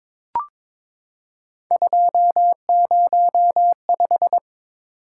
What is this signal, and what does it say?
Morse code, heard as a keyed tone, about 22 wpm, 700 Hz. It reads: 205
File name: practice / 22wpm / Three Letter-Number Combinations - 22wpm